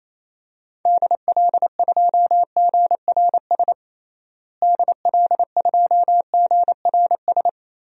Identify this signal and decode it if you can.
Morse code, heard as a keyed tone, about 27 wpm, 700 Hz. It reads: DL2GRS DL2GRS
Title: DXCC-100 Call Signs - 6 Characters - Repeated 3x - 28wpm